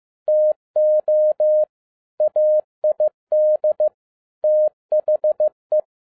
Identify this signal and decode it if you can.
Morse code, heard as a keyed tone, about 15 wpm, 610 Hz. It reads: TO AID THE
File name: 2012-01-mp3 / n13-A NEWLY DESIGNED QSL CARD-15wpm